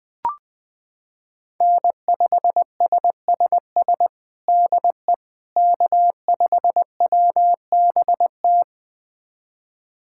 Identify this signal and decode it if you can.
Morse code, heard as a keyed tone, about 21 wpm, 700 Hz. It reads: N5SSS DE K5WBT
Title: USA Callsign DE Callsign - 5 Characters - 20wpm